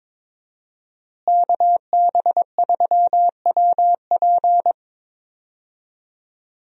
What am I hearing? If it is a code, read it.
Morse code, heard as a keyed tone, about 21 wpm, 700 Hz. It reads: KB3WP
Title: USA Call Signs - 5 Characters - Repeated 2x - 22wpm